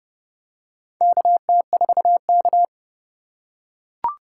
Morse code, heard as a keyed tone, about 30 wpm, 700 Hz. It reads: KT4K
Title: USA Call Signs - 4 Characters - 2x Character Spacing - 30wpm